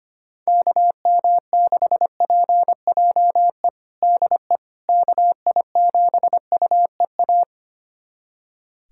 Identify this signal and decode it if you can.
Morse code, heard as a keyed tone, about 25 wpm, 700 Hz. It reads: KM6PJE DE KI7UEA
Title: USA Callsign DE Callsign 25wpm